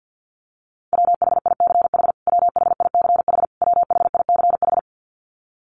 Morse code, heard as a keyed tone, about 50 wpm, 700 Hz. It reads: WHICH WHICH WHICH
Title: Top 300 Words Repeated 50wpm